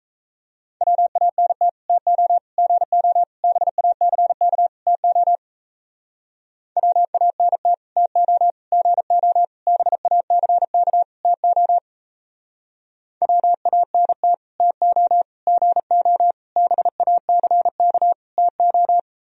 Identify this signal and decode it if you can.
Morse code, heard as a keyed tone, about 37 wpm, 700 Hz. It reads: WANT TO GO BACK TO WANT TO GO BACK TO WANT TO GO BACK TO